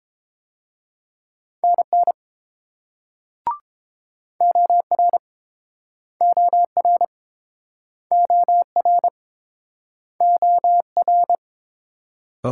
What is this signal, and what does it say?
Morse code, heard as a keyed tone, about 29 wpm, 700 Hz. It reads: NN OR OR OR OR